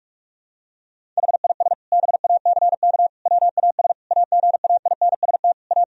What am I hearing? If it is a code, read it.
Morse code, heard as a keyed tone, about 41 wpm, 700 Hz. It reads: HIS BACK WAS AGAINST A
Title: Sentences from Top 1000 Words - Review Part 1 - 45wpm